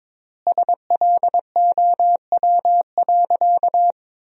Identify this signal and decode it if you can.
Morse code, heard as a keyed tone, about 22 wpm, 700 Hz. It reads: SLOW.
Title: 3x Word-Spacing - Sentences from Top 400 Words - 22wpm